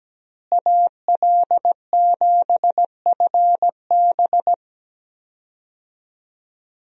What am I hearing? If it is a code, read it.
Morse code, heard as a keyed tone, about 17 wpm, 700 Hz. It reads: AL7FB
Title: USA Call Signs - 5 Characters - Repeated 2x - 17wpm